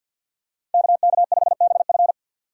Morse code, heard as a keyed tone, about 52 wpm, 700 Hz. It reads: KK5BF
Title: USA Call Signs - 5 Characters - 4x Character Spacing - 50wpm